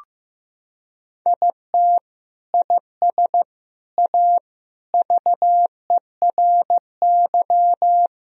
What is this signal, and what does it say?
Morse code, heard as a keyed tone, about 15 wpm, 700 Hz. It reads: IT IS A VERY
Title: Sentences from Top 1000 Words - Review Part 4 - 15wpm